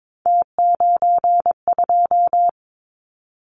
Morse code, heard as a keyed tone, about 22 wpm, 700 Hz. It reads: T92
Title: Callsign Prefixes for Top 50 DXCC - 22wpm